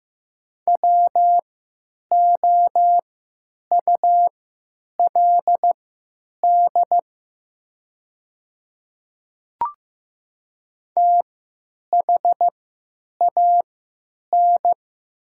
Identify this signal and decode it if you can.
Morse code, heard as a keyed tone, about 15 wpm, spaced as for 9 wpm, 700 Hz. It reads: WOULD THAN
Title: 3x Character Spacing - Top 100 Words - 15wpm